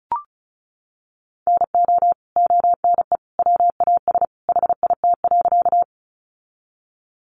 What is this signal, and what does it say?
Morse code, heard as a keyed tone, about 34 wpm, 700 Hz. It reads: NO ONE WAS HIT.
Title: Sentences from Top 1200 Words 35wpm